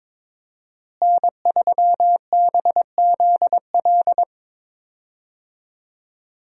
Morse code, heard as a keyed tone, about 22 wpm, 700 Hz. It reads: N3BZL
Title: USA Call Signs - 5 Characters - Repeated 3x - 22wpm